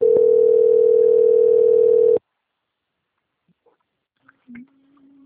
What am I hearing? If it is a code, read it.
Ringback tone (North America), 440 + 480 Hz.